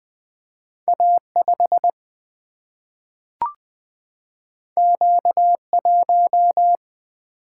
Morse code, heard as a keyed tone, about 20 wpm, 700 Hz. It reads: A5 Q1
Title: Single-Letter Single-Number 20wpm